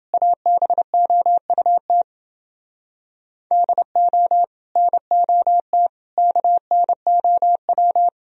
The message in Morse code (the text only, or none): ABOUT DO NOT KNOW